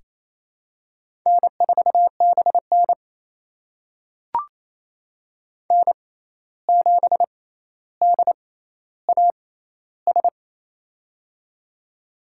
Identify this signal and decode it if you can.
Morse code, heard as a keyed tone, about 28 wpm, spaced as for 13 wpm, 700 Hz. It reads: N4BN N7DAS
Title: USA Call Signs - 6x Character Spacing - 28wpm